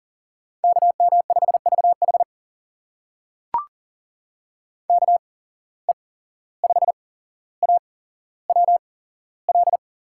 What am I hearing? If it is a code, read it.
Morse code, heard as a keyed tone, about 40 wpm, spaced as for 16 wpm, 700 Hz. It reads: KM5VH KE5AWL
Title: USA Call Signs - 8x Character Spacing - 40wpm